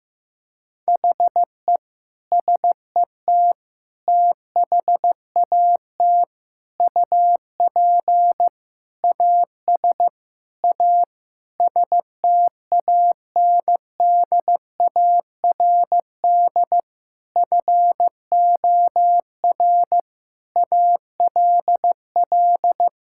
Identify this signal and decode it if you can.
Morse code, heard as a keyed tone, about 15 wpm, 700 Hz. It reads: HE SET THAT UP AS A STANDARD FOR ALL
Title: Sentences from Top 1000 Words - Review Part 6 - 15wpm